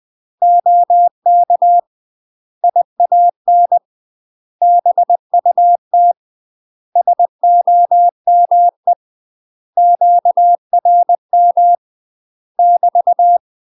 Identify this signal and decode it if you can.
Morse code, heard as a keyed tone, about 20 wpm, 700 Hz. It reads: OK IAN BUT SOME QRM <BT>
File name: UK MASTER QSOs TRAINING / 20WPM / UK MASTER QSOs 0011